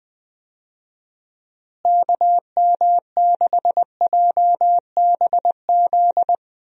Morse code, heard as a keyed tone, about 20 wpm, 700 Hz. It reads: KM6JBZ
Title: USA Call Signs - 6 Characters - Repeated 2x - 20wpm